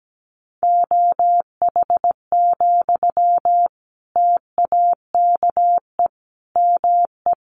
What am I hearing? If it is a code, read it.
Morse code, heard as a keyed tone, about 17 wpm, 700 Hz. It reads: OH, TAKE ME
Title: Sentences from Top 500 Words 17wpm